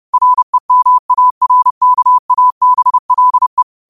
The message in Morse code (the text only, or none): REMARKABLE